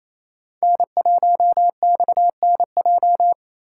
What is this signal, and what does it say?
Morse code, heard as a keyed tone, about 27 wpm, 700 Hz. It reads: N1XNJ
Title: USA Call Signs - 6x Character Spacing - 28wpm